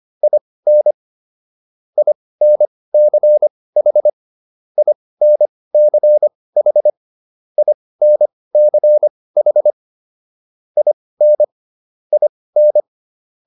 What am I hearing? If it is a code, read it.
Morse code, heard as a keyed tone, about 25 wpm, spaced as for 18 wpm, 600 Hz. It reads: IN INCH INCH INCH IN IN